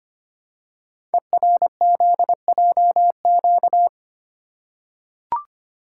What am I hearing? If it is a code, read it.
Morse code, heard as a keyed tone, about 25 wpm, 700 Hz. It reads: ERZJQ
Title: Code Groups - 5 Letters - 25wpm